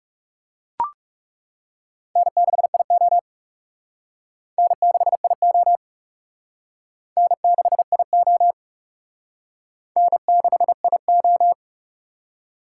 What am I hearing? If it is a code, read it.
Morse code, heard as a keyed tone, about 36 wpm, 700 Hz. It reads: N6IO N6IO N6IO N6IO
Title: Speed Racing - USA Call Signs 30wpm